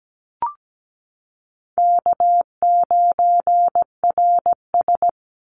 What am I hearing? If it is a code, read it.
Morse code, heard as a keyed tone, about 17 wpm, 700 Hz. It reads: K9RS
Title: Call Signs - 17wpm